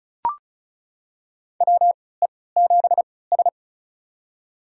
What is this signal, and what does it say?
Morse code, heard as a keyed tone, about 36 wpm, spaced as for 20 wpm, 700 Hz. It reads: WE7S
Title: USA Call Signs - 4 Characters - 3x Character Spacing - 35wpm